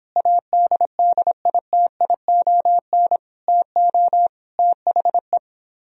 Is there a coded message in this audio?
Yes: Morse code from 0.2 to 5.4 s, heard as a keyed tone, about 26 wpm, 700 Hz. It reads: ADDITION TO THE